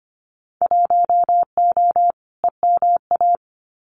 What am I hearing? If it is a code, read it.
Morse code, heard as a keyed tone, about 25 wpm, 700 Hz. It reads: 1O EMA
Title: Winter Field Day Exchanges 25wpm